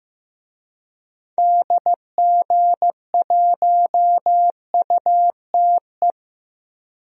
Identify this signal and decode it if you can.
Morse code, heard as a keyed tone, about 15 wpm, 700 Hz. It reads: DG1UTE
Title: DXCC-100 Call Signs - 6 Characters - 15wpm